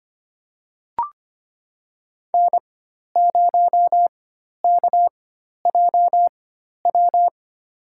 Morse code, heard as a keyed tone, about 25 wpm, spaced as for 12 wpm, 700 Hz. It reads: N0KJW
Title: USA Call Signs - 4x Character Spacing - 25wpm